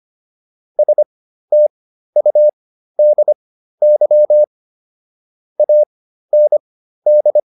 Morse code, heard as a keyed tone, about 25 wpm, spaced as for 13 wpm, 600 Hz. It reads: STUDY AND
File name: PR402 13